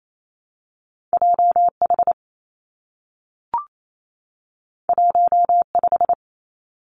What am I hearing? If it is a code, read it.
Morse code, heard as a keyed tone, about 28 wpm, 700 Hz. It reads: J H 1 5
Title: Two Challenging Letter-Number Combinations 28wpm